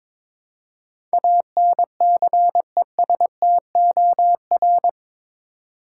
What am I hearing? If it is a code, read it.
Morse code, heard as a keyed tone, about 22 wpm, 700 Hz. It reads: ANCESTOR